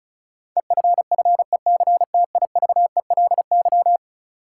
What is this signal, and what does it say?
Morse code, heard as a keyed tone, about 36 wpm, 700 Hz. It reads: EFFECTIVELY
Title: Sentences from Top 1400 Words - New Word Drill - 35wpm